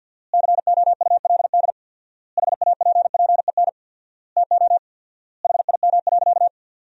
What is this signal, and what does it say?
Morse code, heard as a keyed tone, about 50 wpm, 700 Hz. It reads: COULD HAPPEN TO HIM.
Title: 4x Word-Spacing - Sentences from Top 400 Words - 50wpm